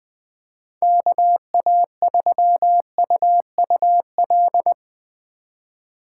Morse code, heard as a keyed tone, about 20 wpm, 700 Hz. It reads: KA3UUL
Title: USA Call Signs - 6 Characters - 4x Character Spacing - 20wpm